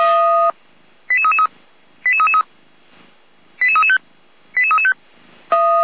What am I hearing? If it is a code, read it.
Five-tone selective call (ZVEI), heard as quick runs of five tones, tones of about 70 ms. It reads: 88333 88335 88335